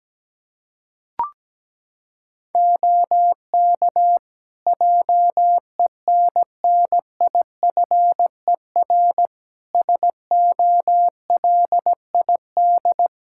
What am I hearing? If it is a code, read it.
Morse code, heard as a keyed tone, about 17 wpm, 700 Hz. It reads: OK JENNIFER SOLID
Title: Practice QSO - Exchange 3 - More Info - 17wpm